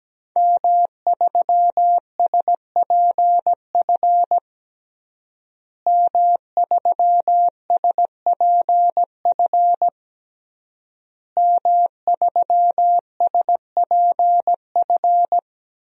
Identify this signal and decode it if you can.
Morse code, heard as a keyed tone, about 17 wpm, 700 Hz. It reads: M3SPF M3SPF M3SPF
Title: DXCC-100 Call Signs - 5 Characters - Repeated 3x - 17wpm